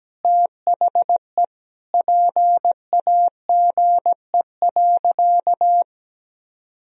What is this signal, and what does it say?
Morse code, heard as a keyed tone, about 17 wpm, 700 Hz. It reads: THE PAGE.
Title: Sentences from Top 1000 Words - Review Part 1 - 17wpm